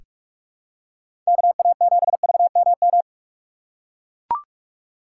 Morse code, heard as a keyed tone, about 44 wpm, 700 Hz. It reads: KA7VMM
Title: USA Call Signs - 6x Character Spacing - 45wpm